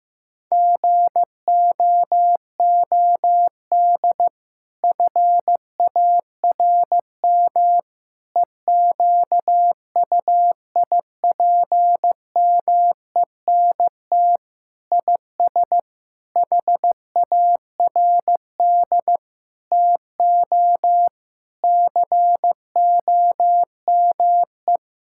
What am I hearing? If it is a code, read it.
Morse code, heard as a keyed tone, about 15 wpm, 700 Hz. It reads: GOOD FARM EQUIPMENT IS HARD TO COME
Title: Sentences from Top 1000 Words - Review Part 5 - 15wpm